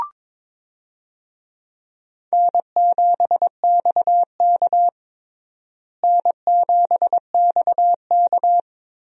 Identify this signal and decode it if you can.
Morse code, heard as a keyed tone, about 22 wpm, 700 Hz. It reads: N7XK N7XK
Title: USA Call Signs - 4 Characters - Repeated 3x - 22wpm